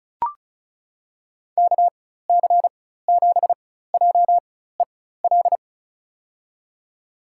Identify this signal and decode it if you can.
Morse code, heard as a keyed tone, about 34 wpm, spaced as for 16 wpm, 700 Hz. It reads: KC7JEL
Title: USA Call Signs - 4x Character Spacing - 35wpm